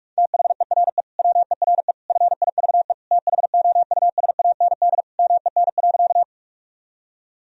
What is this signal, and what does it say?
Morse code, heard as a keyed tone, about 41 wpm, 700 Hz. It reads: THERE WERE FIVE THOUSAND MEN.
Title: Sentences from Top 1000 Words - Review Part 3 - 45wpm